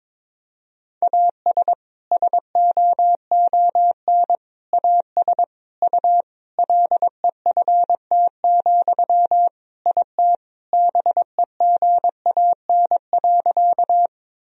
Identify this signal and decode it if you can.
Morse code, heard as a keyed tone, about 21 wpm, 700 Hz. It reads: AS SOON AS U LEFT, IT BEGAN.
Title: Sentences from Top 400 Words and Abbreviations 22wpm